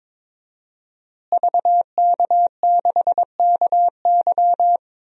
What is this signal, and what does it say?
Morse code, heard as a keyed tone, about 21 wpm, 700 Hz. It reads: VK6KY
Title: DXCC-100 Call Signs - 5 Characters - Repeated 2x - 22wpm